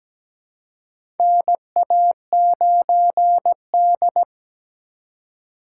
Morse code, heard as a keyed tone, about 17 wpm, 700 Hz. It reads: NA9D
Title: USA Call Signs - 4 Characters - 4x Character Spacing - 17wpm